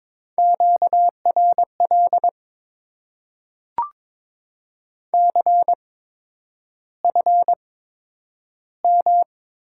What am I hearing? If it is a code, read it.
Morse code, heard as a keyed tone, about 22 wpm, spaced as for 9 wpm, 700 Hz. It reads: QRL CFM
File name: Top 90 QSO Elements - 8x Character Spacing - 22wpm